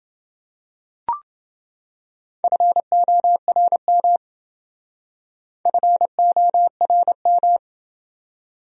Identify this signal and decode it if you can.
Morse code, heard as a keyed tone, about 28 wpm, 700 Hz. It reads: FORM FORM